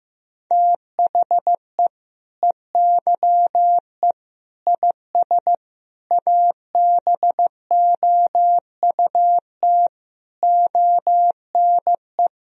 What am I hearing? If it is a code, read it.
Morse code, heard as a keyed tone, about 15 wpm, 700 Hz. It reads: THE EYE IS ABOUT ONE
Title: Sentences from Top 1000 Words - Review Part 3 - 15wpm